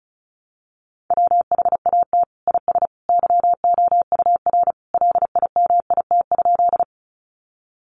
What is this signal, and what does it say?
Morse code, heard as a keyed tone, about 34 wpm, 700 Hz. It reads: WHAT IS YOUR LIMIT?
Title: Sentences from Top 1500 Words 35wpm